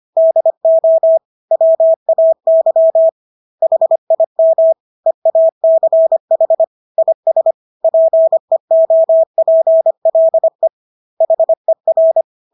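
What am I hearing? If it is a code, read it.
Morse code, heard as a keyed tone, about 25 wpm, 650 Hz. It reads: DO WAY HIM EACH IS PEOPLE HER
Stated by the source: Morse practice of English words at 25 wpm from CW med Gnister.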